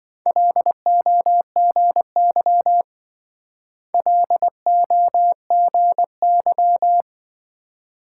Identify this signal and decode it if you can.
Morse code, heard as a keyed tone, about 26 wpm, 700 Hz. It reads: LOGY LOGY